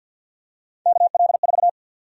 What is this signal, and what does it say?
Morse code, heard as a keyed tone, about 52 wpm, 700 Hz. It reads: KL4
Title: USA Call Sign Prefixes - 50wpm